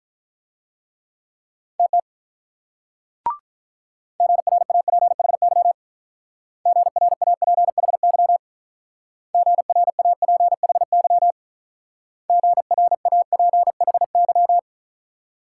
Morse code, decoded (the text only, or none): TT GRAPHY GRAPHY GRAPHY GRAPHY